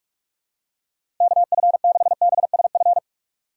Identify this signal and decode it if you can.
Morse code, heard as a keyed tone, about 46 wpm, 700 Hz. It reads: KF6BSF
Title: USA Call Signs - 6 Characters - 4x Character Spacing - 45wpm